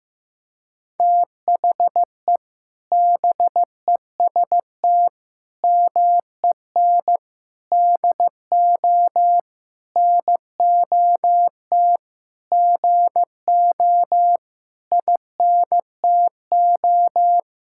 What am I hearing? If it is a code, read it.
Morse code, heard as a keyed tone, about 15 wpm, 700 Hz. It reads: THE BEST MEN DO NOT GO INTO
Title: Sentences from Top 1000 Words - Review Part 6 - 15wpm